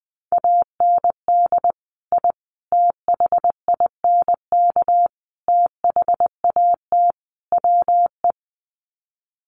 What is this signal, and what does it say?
Morse code, heard as a keyed tone, about 20 wpm, 700 Hz. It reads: AND I THINK THAT WE
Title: Sets of 5 Words Encore2 20wpm